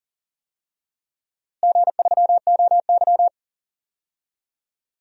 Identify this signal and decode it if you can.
Morse code, heard as a keyed tone, about 40 wpm, 700 Hz. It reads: G3OY
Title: DXCC-100 Call Signs - 4 Characters - Repeated 3x - 40wpm